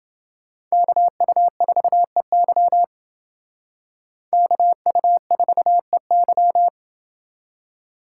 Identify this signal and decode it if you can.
Morse code, heard as a keyed tone, about 28 wpm, 700 Hz. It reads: KU4EY KU4EY